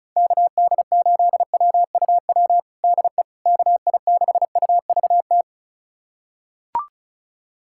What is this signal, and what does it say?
Morse code, heard as a keyed tone, about 36 wpm, 700 Hz. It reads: KD8WUW DE KI6UVT
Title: USA Callsign DE Callsign - 6 Characters - 35wpm